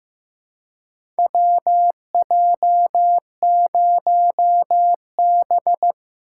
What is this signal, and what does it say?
Morse code, heard as a keyed tone, about 15 wpm, 700 Hz. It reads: WJ0B
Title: USA Call Signs - 4 Characters - 3x Character Spacing - 15wpm